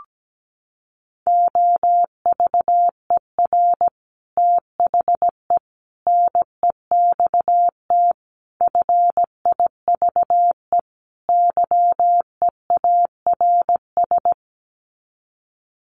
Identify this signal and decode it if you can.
Morse code, heard as a keyed tone, about 17 wpm, 700 Hz. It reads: OVER THE NEXT FIVE YEARS
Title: Sets of 5 Words 17wpm